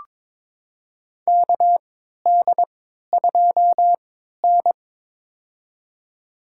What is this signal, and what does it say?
Morse code, heard as a keyed tone, about 22 wpm, spaced as for 12 wpm, 700 Hz. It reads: KD2N
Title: USA Call Signs - 4 Characters - 3x Character Spacing - 22wpm